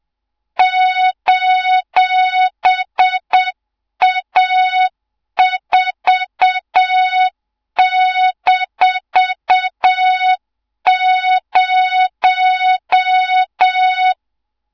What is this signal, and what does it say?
Morse code, heard as a keyed tone, about 7 wpm, 3700 Hz, with noise. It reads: :A4-0
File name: Morze-HA2MN / hang / 24 GYAKORLAT